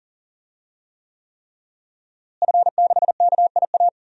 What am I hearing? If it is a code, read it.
Morse code, heard as a keyed tone, about 40 wpm, 700 Hz. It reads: F6KIA
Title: DXCC-100 Call Signs - 5 Characters - Repeated 3x - 40wpm